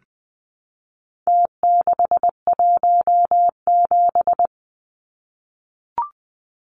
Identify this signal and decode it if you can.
Morse code, heard as a keyed tone, about 20 wpm, 700 Hz. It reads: T617
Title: Numbers - Contest Style - 4-Digit 20wpm